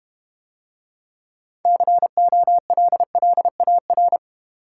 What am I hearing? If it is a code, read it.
Morse code, heard as a keyed tone, about 31 wpm, 700 Hz. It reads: COLLAR
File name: Easy Words - Speed-Racing with Hints - 25wpm